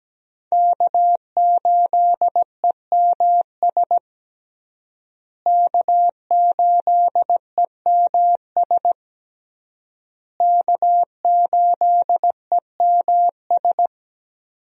Morse code, heard as a keyed tone, about 17 wpm, 700 Hz. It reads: K8EMS K8EMS K8EMS
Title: USA Call Signs - 5 Characters - Repeated 3x - 17wpm